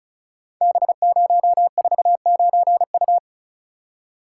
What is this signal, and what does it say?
Morse code, heard as a keyed tone, about 36 wpm, 700 Hz. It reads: B049U
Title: Code Groups - 5 Letter:Numbers - 35wpm